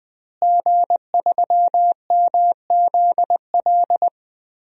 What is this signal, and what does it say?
Morse code, heard as a keyed tone, about 20 wpm, 700 Hz. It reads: G3MZL
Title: DXCC-100 Call Signs - 5 Characters - Repeated 3x - 20wpm